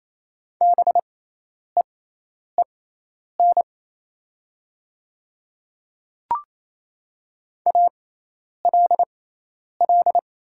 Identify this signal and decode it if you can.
Morse code, heard as a keyed tone, about 27 wpm, spaced as for 10 wpm, 700 Hz. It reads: BEEN ALL
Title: 6x Character Spacing - Top 100 Words - 28wpm